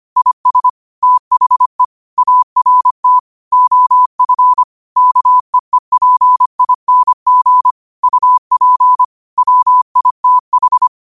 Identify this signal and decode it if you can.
Morse code, heard as a keyed tone, about 25 wpm, 1000 Hz. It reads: IS THE ART OF KEEPING UP WITH
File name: qotdfiles / 25wpm